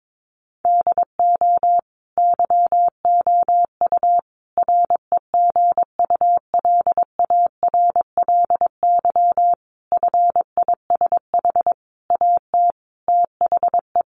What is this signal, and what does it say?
Morse code, heard as a keyed tone, about 22 wpm, 700 Hz. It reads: DO YOU REGULARLY FISH AT THE
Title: Sentences from Top 1400 Words 22wpm